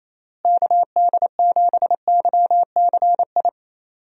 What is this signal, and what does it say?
Morse code, heard as a keyed tone, about 28 wpm, 700 Hz. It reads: KD7YCI
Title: USA Call Signs - 6 Characters - Repeated 2x - 28wpm